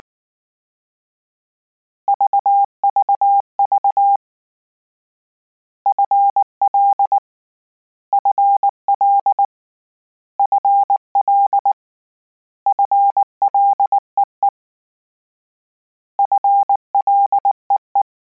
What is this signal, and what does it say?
Morse code, heard as a keyed tone, about 19 wpm, 800 Hz. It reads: VVV FL FL FL FLEE FLEE